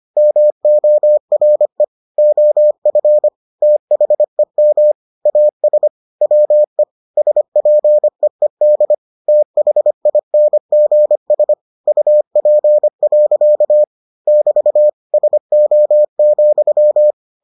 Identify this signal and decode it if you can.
Morse code, heard as a keyed tone, about 25 wpm, 600 Hz. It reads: MORE OF THEM AS WE SPEED THINGS UP. <BT> SO,